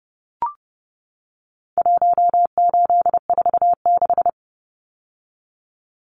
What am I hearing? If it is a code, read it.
Morse code, heard as a keyed tone, about 30 wpm, 700 Hz. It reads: 1846
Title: Numbers - Contest Style - 4-Digit 30wpm